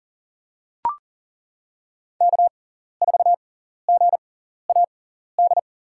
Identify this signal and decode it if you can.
Morse code, heard as a keyed tone, about 40 wpm, spaced as for 14 wpm, 700 Hz. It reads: K4GAD
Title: USA Call Signs - 6x Character Spacing - 40wpm